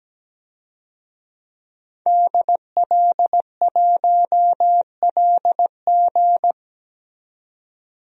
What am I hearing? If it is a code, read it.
Morse code, heard as a keyed tone, about 17 wpm, 700 Hz. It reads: DL1LG
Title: DXCC-100 Call Signs - 5 Characters - 17wpm